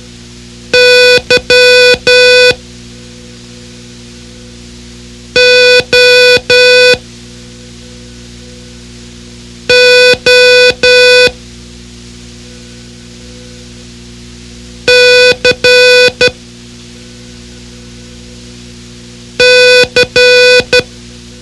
Morse code, heard as a keyed tone, about 12 wpm, 500 Hz. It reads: Y O O C C
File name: ham radio / CodeQuick / Code Quick 04